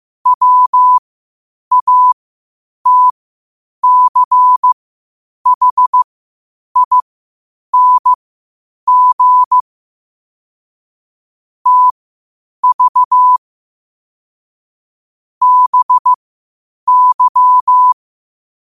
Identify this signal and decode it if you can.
Morse code, heard as a keyed tone, about 15 wpm, spaced as for 8 wpm, 1000 Hz. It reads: WATCHING TV BY
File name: qotdfiles / 8wpm